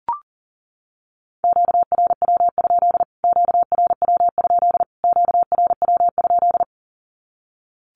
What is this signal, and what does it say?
Morse code, heard as a keyed tone, about 40 wpm, 700 Hz. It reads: QRW? QRW? QRW?
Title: Q-Codes Repeated 3x 40wpm